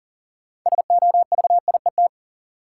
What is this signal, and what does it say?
Morse code, heard as a keyed tone, about 40 wpm, 700 Hz. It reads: SOVIET
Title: Sets of 5 Words Encore - New Word Drill - 40wpm